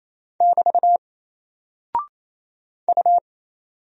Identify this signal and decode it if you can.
Morse code, heard as a keyed tone, about 27 wpm, 700 Hz. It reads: <BT> U
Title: Single Character (Letter:Number:Punctuation:Prosign) 28wpm